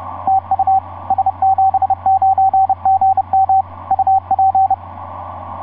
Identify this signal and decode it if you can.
Morse code, heard as a keyed tone, about 30 wpm, 770 Hz, with radio noise. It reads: TU S79GM UP
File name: s79gm f6eyb etc 12m cw 24mar2010 1207